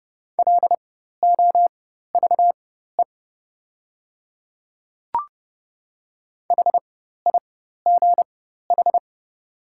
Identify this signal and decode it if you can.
Morse code, heard as a keyed tone, about 30 wpm, spaced as for 14 wpm, 700 Hz. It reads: LOVE HIGH
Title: 4x Character Spacing - Top 500 Words - 30wpm